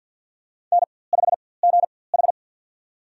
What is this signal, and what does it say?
Morse code, heard as a keyed tone, about 48 wpm, spaced as for 23 wpm, 700 Hz. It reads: N5GH
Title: USA Call Signs - 4x Character Spacing - 50wpm